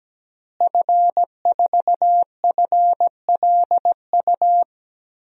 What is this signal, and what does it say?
Morse code, heard as a keyed tone, about 17 wpm, 700 Hz. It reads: F4FLU
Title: DXCC-100 Call Signs - 5 Characters - Repeated 3x - 17wpm